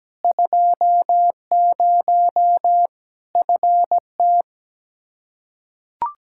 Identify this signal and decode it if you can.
Morse code, heard as a keyed tone, about 17 wpm, 700 Hz. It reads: 20 FT
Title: Practice QSO - Exchange 2 - Info - 17wpm